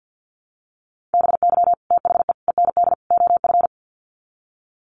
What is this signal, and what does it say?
Morse code, heard as a keyed tone, about 50 wpm, 700 Hz. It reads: BY THE END OF
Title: Sets of 4 Words 50wpm